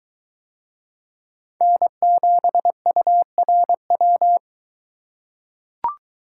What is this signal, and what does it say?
Morse code, heard as a keyed tone, about 21 wpm, 700 Hz. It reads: N7URW